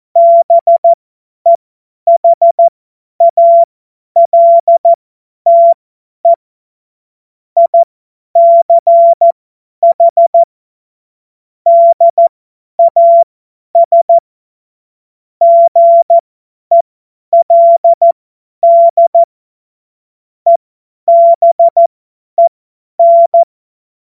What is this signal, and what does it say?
Morse code, heard as a keyed tone, about 14 wpm, spaced as for 10 wpm, 680 Hz. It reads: BEHALTE ICH DAS GELD EBEN
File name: Morsekurs / Witze / Witz 20. Januar 2026 50 F70